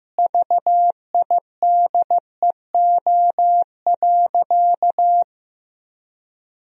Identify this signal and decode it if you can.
Morse code, heard as a keyed tone, about 15 wpm, 700 Hz. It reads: VIDEO.
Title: Sentences from Top 1000 Words - Review Part 4 - 15wpm